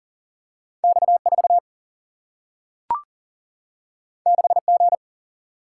Morse code, heard as a keyed tone, about 38 wpm, 700 Hz. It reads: X 4 6 G
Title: Single-Letter Single-Number 40wpm